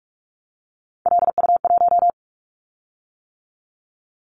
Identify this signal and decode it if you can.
Morse code, heard as a keyed tone, about 46 wpm, 700 Hz. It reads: LU1
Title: Callsign Prefixes for Top 50 DXCC - 45wpm